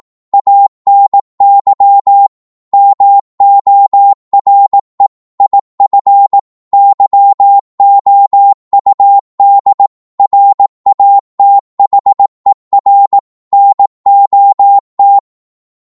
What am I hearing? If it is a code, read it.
Morse code, heard as a keyed tone, about 18 wpm, 800 Hz. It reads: ANY MORE IF YOUD RATHER NOT
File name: Morse / psg-18wpm